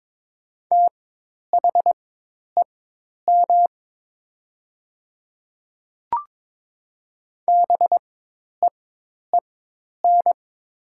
Morse code, heard as a keyed tone, about 22 wpm, spaced as for 10 wpm, 700 Hz. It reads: THEM BEEN
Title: 4x Character Spacing - Top 200 Words - 22wpm